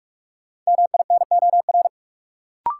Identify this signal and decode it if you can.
Morse code, heard as a keyed tone, about 44 wpm, 700 Hz. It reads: MINOR
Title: Sentences from Top 1500 Words - New Word Drill Alt - 45wpm